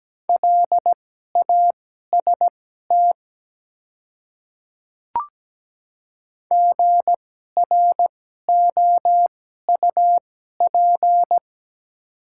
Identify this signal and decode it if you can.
Morse code, heard as a keyed tone, about 17 wpm, spaced as for 12 wpm, 700 Hz. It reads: LAST GROUP
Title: 2x Character Spacing - Top 300 Words - 17wpm